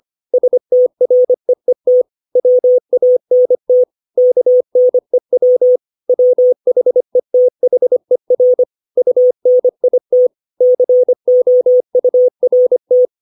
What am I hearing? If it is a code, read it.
Morse code, heard as a keyed tone, about 25 wpm, 500 Hz. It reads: STREET WANT KNEW WHETHER UNIT COURT
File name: 632 words / 632 Words 25WPM